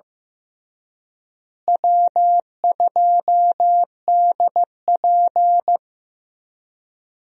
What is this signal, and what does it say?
Morse code, heard as a keyed tone, about 15 wpm, 700 Hz. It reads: W2DP
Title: USA Call Signs - 4 Characters - Repeated 3x - 15wpm